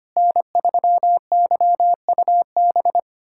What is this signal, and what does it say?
Morse code, heard as a keyed tone, about 25 wpm, 700 Hz. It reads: N3YUB
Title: USA Callsign DE Callsign - 5 Characters - 25wpm